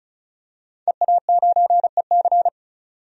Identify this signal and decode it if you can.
Morse code, heard as a keyed tone, about 36 wpm, 700 Hz. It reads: EA9EC
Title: DXCC-100 Call Signs - 5 Characters - 35wpm